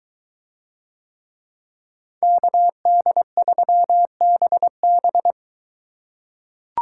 Morse code, heard as a keyed tone, about 23 wpm, 700 Hz. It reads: KD3BB